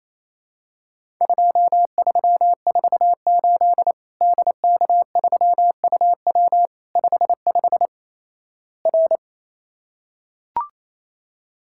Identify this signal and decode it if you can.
Morse code, heard as a keyed tone, about 28 wpm, 700 Hz. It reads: 2348 DK3UW 55 R
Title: Worked All Europe DX Contest - QTC Exchange - 28wpm